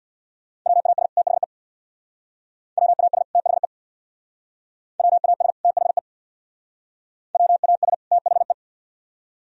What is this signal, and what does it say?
Morse code, heard as a keyed tone, about 58 wpm, 700 Hz. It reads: WAS THE WAS THE WAS THE WAS THE
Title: Speed-Racing - Sets of 2 Words 50wpm